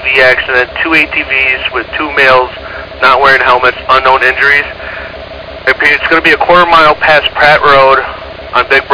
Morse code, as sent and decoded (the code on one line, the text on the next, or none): none